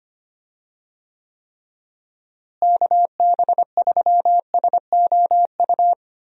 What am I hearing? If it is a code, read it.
Morse code, heard as a keyed tone, about 25 wpm, 700 Hz. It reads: KB3SOU
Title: USA Call Signs - 6 Characters - Repeated 3x - 25wpm